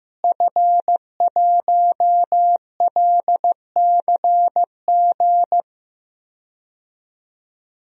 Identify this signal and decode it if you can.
Morse code, heard as a keyed tone, about 15 wpm, 700 Hz. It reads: F1LCG
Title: DXCC-100 Call Signs - 5 Characters - Repeated 3x - 15wpm